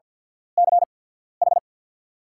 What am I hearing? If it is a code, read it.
Morse code, heard as a keyed tone, about 48 wpm, spaced as for 14 wpm, 690 Hz. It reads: CH
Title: 8x Character Spacing - Top 100 Words - 50wpm